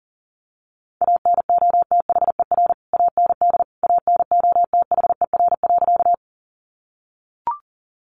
Morse code, heard as a keyed tone, about 40 wpm, 700 Hz. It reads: ANOTHER AND ANOTHER.
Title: Sentences from Top 600 Words 40wpm